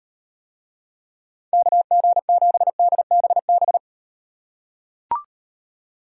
Morse code, heard as a keyed tone, about 38 wpm, 700 Hz. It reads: KG7DBB